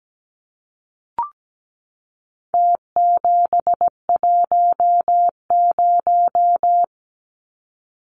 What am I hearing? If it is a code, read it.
Morse code, heard as a keyed tone, about 17 wpm, 700 Hz. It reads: T710
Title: Numbers - Contest Style - 4-Digit 17wpm